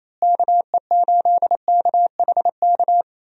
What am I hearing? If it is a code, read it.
Morse code, heard as a keyed tone, about 28 wpm, 700 Hz. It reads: KE8KHK
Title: USA Call Signs - 6 Characters - Repeated 3x - 28wpm